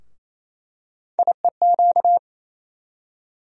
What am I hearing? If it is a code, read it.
Morse code, heard as a keyed tone, about 27 wpm, 700 Hz. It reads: IEQ
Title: USA Call Sign Postfixes - 3 Letters 28wpm